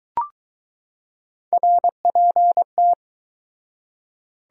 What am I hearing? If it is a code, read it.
Morse code, heard as a keyed tone, about 20 wpm, 700 Hz. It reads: RPT